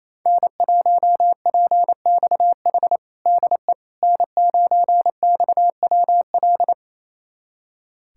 Morse code, heard as a keyed tone, about 27 wpm, 700 Hz. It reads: N1PXH DE N9XWL
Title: USA Callsign DE Callsign - 5 Characters - 28wpm